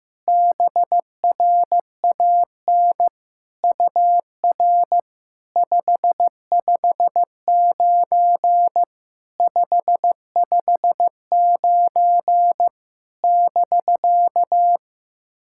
Morse code, heard as a keyed tone, about 15 wpm, 700 Hz. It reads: BRAN UR 559 559 <BK>
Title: SOTA QSO Practice 15wpm